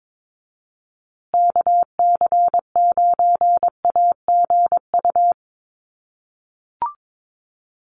Morse code, heard as a keyed tone, about 22 wpm, 700 Hz. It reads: KC9AGU
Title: USA Call Signs Repeated 2x 22wpm